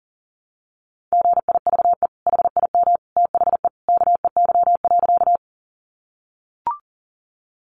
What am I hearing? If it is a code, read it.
Morse code, heard as a keyed tone, about 40 wpm, 700 Hz. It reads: GIVE HIM THE KEY.
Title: Sentences from Top 600 Words 40wpm